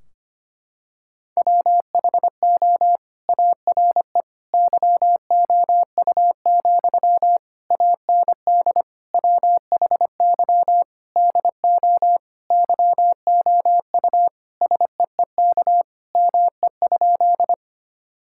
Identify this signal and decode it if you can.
Morse code, heard as a keyed tone, about 25 wpm, 700 Hz. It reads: WHO ARE YOU, AND WHY DO YOU SEEK ME?
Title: Sentences from Top 1600 Words 25wpm